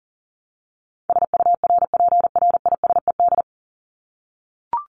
Morse code, heard as a keyed tone, about 40 wpm, 700 Hz. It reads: SURPRISED
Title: Top 1501-1750 Words 40wpm